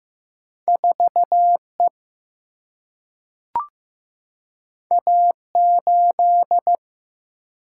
Morse code, heard as a keyed tone, about 15 wpm, 700 Hz. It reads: 4E A8
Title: Single-Letter Single-Number 15wpm